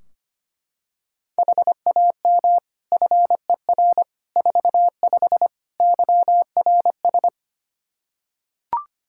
Morse code, heard as a keyed tone, about 25 wpm, 700 Hz. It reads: HAM FER 45 YRS
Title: Practice QSO - Exchange 3 - More Info - 25wpm